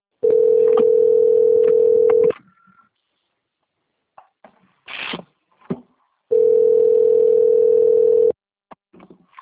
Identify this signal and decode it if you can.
Ringback tone (North America), 440 + 480 Hz.